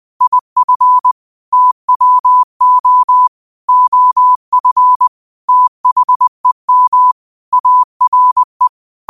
Morse code, heard as a keyed tone, about 20 wpm, 1000 Hz. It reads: IF TWO OF THEM ARE
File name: qotdfiles / 20wpm